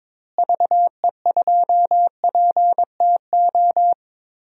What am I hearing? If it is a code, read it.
Morse code, heard as a keyed tone, about 22 wpm, 700 Hz. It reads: VE2PTO
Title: DXCC-100 Call Signs - 6 Characters - Repeated 3x - 22wpm